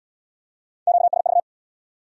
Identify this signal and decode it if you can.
Morse code, heard as a keyed tone, about 75 wpm, 700 Hz. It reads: QSV